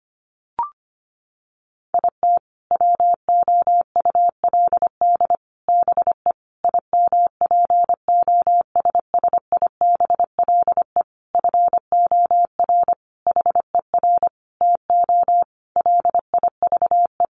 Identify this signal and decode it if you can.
Morse code, heard as a keyed tone, about 25 wpm, 700 Hz. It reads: IT WOULD BE IMPOSSIBLE FOR HER TO LIVE
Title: Sentences from Top 1100 Words 25wpm